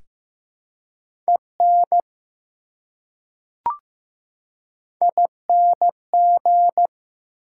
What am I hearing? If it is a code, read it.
Morse code, heard as a keyed tone, about 15 wpm, 700 Hz. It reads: EN ING
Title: Prefixes and Suffixes 15wpm